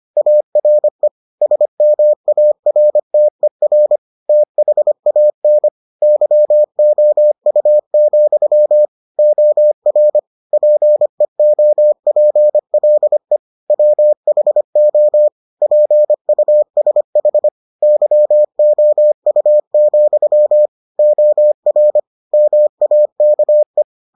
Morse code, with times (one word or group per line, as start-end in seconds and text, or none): 0.2-1.1 ARE
1.4-4.0 SMARTER
4.3-5.7 THAN
6.0-8.9 YOU,
9.2-10.2 OR
10.5-13.4 PEOPLE
13.7-15.3 WHO
15.6-17.5 PUSH
17.8-20.7 YOU,
21.0-22.0 OR
22.3-23.8 MAKE